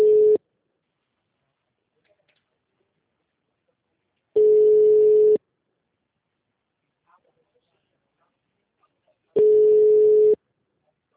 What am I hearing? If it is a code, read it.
Ringback tone (Europe, 425 Hz).
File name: out- -2016- - - .8